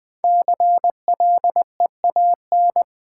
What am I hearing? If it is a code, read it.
Morse code, heard as a keyed tone, about 20 wpm, 700 Hz. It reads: CLEAN
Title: Sentences from Top 1100 Words - New Word Drill - 20wpm